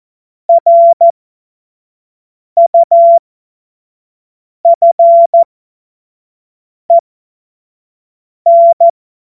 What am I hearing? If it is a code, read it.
Morse code, heard as a keyed tone, about 14 wpm, spaced as for 5 wpm, 680 Hz. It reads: RUFEN
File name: Morsekurs / Witze / Witz 20. Juli 2021, 25 F70